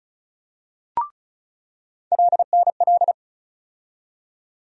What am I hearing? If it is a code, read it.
Morse code, heard as a keyed tone, about 36 wpm, 700 Hz. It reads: LNL
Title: USA Call Sign Postfixes - 3 Letters 35wpm